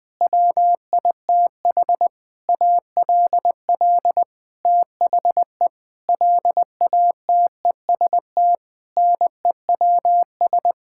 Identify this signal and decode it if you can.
Morse code, heard as a keyed tone, about 20 wpm, 700 Hz. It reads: WITH ALL THE LATEST NEWS
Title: Sentences with 9-Word Phrases - Spoken Beginning - 20wpm